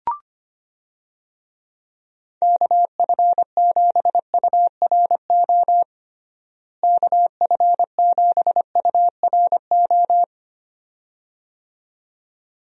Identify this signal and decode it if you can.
Morse code, heard as a keyed tone, about 25 wpm, 700 Hz. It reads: KF7URO KF7URO
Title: USA Call Signs - 6 Characters - Repeated 2x - 25wpm